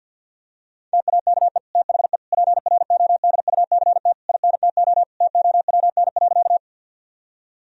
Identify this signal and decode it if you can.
Morse code, heard as a keyed tone, about 50 wpm, 700 Hz. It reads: TAKE THE PRODUCT INTO TOWN.
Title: Sentences from Top 1000 Words - Review Part 2 - 50wpm